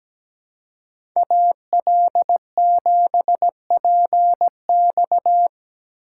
Morse code, heard as a keyed tone, about 17 wpm, 700 Hz. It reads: AL7PX
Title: USA Call Signs - 2x Character Spacing - 17wpm